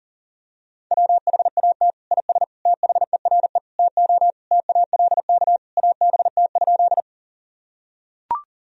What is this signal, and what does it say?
Morse code, heard as a keyed tone, about 40 wpm, 700 Hz. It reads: WHAT IS THERE TO TALK ABT?
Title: Sentences from Top 400 Words and Abbreviations 40wpm